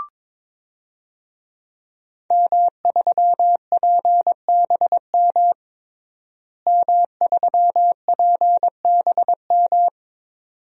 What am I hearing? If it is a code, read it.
Morse code, heard as a keyed tone, about 22 wpm, 700 Hz. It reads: M3PBM M3PBM
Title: DXCC-100 Call Signs - 5 Characters - Repeated 2x - 22wpm